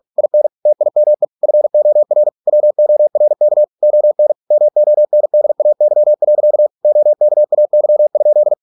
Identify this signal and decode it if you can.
Morse code, heard as a keyed tone, about 46 wpm, 600 Hz. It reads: IN TIME FOR WORK ON MONDAY. OKAY?